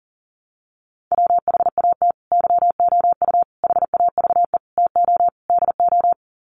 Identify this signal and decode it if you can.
Morse code, heard as a keyed tone, about 40 wpm, 700 Hz. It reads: WHAT YOU HAVE TO DO
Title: Sets of 5 Words Encore 40wpm